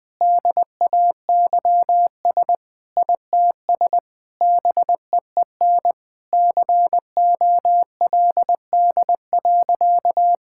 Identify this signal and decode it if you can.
Morse code, heard as a keyed tone, about 20 wpm, 700 Hz. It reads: DAYS ITS BEEN COLD.
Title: Sentences with 3-Word Phrases - 20wpm